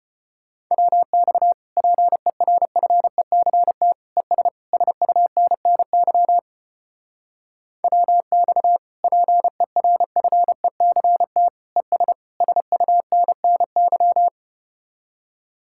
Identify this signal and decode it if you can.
Morse code, heard as a keyed tone, about 33 wpm, 700 Hz. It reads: WX PERFECT ES SUNNY WX PERFECT ES SUNNY